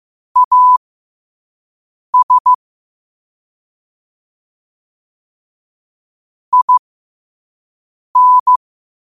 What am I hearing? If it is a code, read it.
Morse code, heard as a keyed tone, about 15 wpm, spaced as for 5 wpm, 1000 Hz. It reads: AS IN